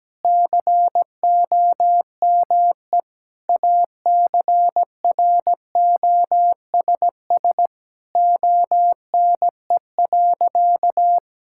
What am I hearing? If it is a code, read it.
Morse code, heard as a keyed tone, about 17 wpm, 700 Hz. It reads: COME ACROSS ONE.
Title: Sentences from Top 1000 Words - Review Part 5 - 17wpm